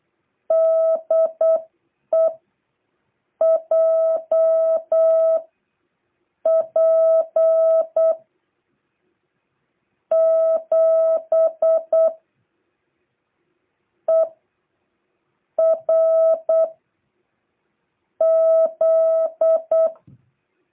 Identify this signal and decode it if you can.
Morse code, heard as a keyed tone, about 8 wpm, spaced as for 4 wpm, 660 Hz. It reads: DE JP7ERZ